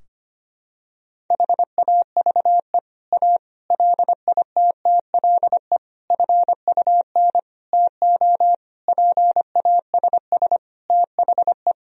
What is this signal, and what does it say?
Morse code, heard as a keyed tone, about 25 wpm, 700 Hz. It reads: HAVE A LITTLE FUN TO PASS THE
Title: Sentences from Top 1000 Words - Review Part 4 - 25wpm